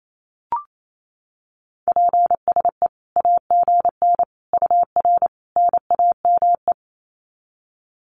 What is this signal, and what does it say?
Morse code, heard as a keyed tone, about 28 wpm, 700 Hz. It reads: PSE AGN UR NAME
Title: Common QSO Phrases 28wpm